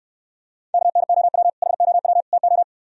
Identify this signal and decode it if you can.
Morse code, heard as a keyed tone, about 69 wpm, 700 Hz. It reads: KNOW HOW TO